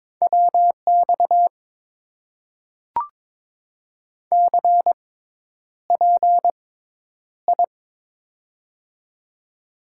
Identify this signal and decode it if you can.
Morse code, heard as a keyed tone, about 22 wpm, spaced as for 10 wpm, 700 Hz. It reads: WX CPI
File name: Top 90 QSO Elements - 6x Character Spacing - 22wpm